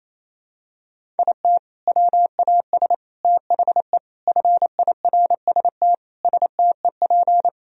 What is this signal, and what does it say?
Morse code, heard as a keyed tone, about 28 wpm, 700 Hz. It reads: IT WAS THE FIRST STEP
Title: Sentences from Top 1000 Words - Review Part 4 - 28wpm